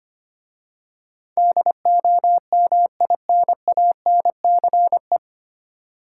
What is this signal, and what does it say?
Morse code, heard as a keyed tone, about 25 wpm, 700 Hz. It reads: DOMINANCE
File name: practice / 25wpm / 9 Letter Words - First 3 Letters Given - 25wpm